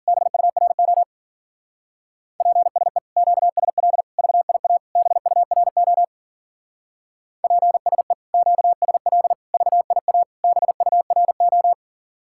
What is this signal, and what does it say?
Morse code, heard as a keyed tone, about 51 wpm, 700 Hz. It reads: BURO PSE QSL VIA BURO PSE QSL VIA BURO